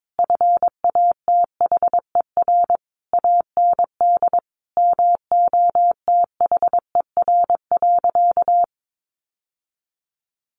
Morse code, heard as a keyed tone, about 22 wpm, 700 Hz. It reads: FATHER AND MOTHER.
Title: Sentences from Top 1100 Words 22wpm